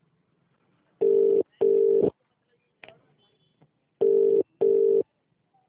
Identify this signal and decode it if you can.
Ringback tone (UK), 400 + 450 Hz.